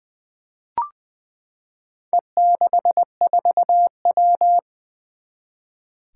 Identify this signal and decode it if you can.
Morse code, heard as a keyed tone, about 20 wpm, 700 Hz. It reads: E64W
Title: Four Challenging Letter-Number Combinations 20wpm